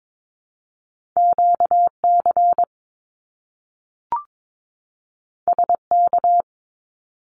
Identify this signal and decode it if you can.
Morse code, heard as a keyed tone, about 22 wpm, 700 Hz. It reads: QC SK
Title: Canadian Provinces and Territories 22wpm